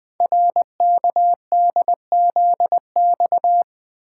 Morse code, heard as a keyed tone, about 20 wpm, 700 Hz. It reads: RKDZX
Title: Code Groups - 5 Letters - 20wpm